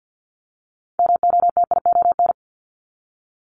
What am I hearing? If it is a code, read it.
Morse code, heard as a keyed tone, about 50 wpm, 700 Hz. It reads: MOTION
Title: Multi-letter Phonemes Repeated 3x 50wpm